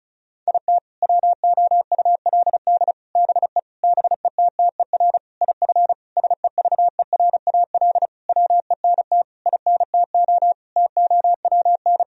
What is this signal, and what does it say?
Morse code, heard as a keyed tone, about 34 wpm, 700 Hz. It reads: IT WOULD BE BETTER IF SEVERAL WENT INTO TOWN
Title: Sentences from Top 1000 Words - Review Part 2 - 35wpm